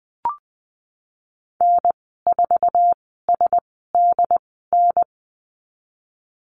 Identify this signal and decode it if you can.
Morse code, heard as a keyed tone, about 20 wpm, spaced as for 14 wpm, 700 Hz. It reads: N4SDN
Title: 076 Lesson 13g - Letter L - Call Signs 20wpm